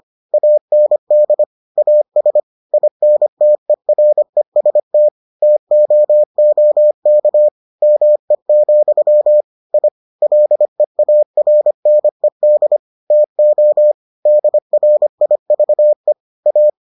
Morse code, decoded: AND AS INTEREST TOOK ME, I LEARNED TO DRIVE A

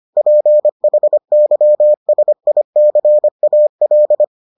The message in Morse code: PHYSICAL

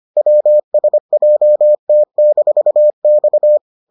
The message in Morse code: WSJT-X

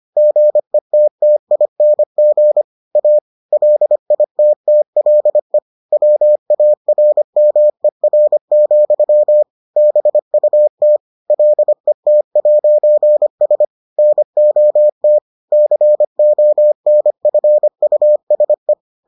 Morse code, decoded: GETTING A LITTLE WARMER, BUT LET'S NOT CONFUSE

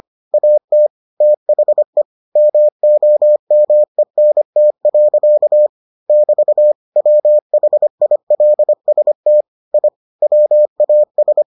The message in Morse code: AT THE MOMENT. <BT> WHILST I WAS

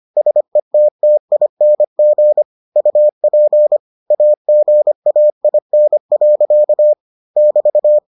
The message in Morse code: SETTING UP AGAIN. <BT>

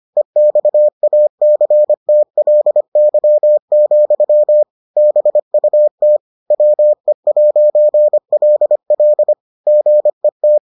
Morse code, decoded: EXACTLY, BUT WE'LL GET

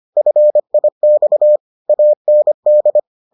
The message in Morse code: FIX AND

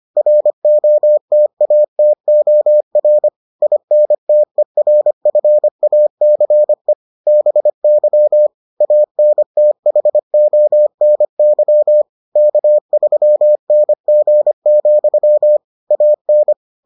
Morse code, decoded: ROTATOR INTERFACE BY ANTHONY K3NG, AN

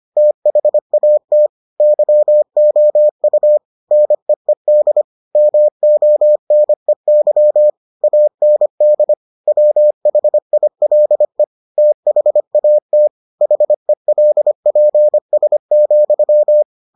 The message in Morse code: THAT YOU NEED MONEY AND WHILE THAT HELPS,

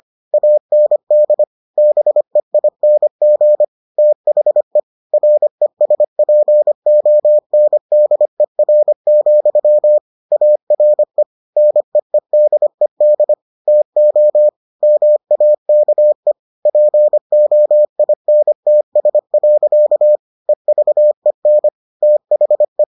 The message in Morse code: AND BEING THE RESPONDER, ARE NEEDED TO MAKE POINTS. EVEN THE

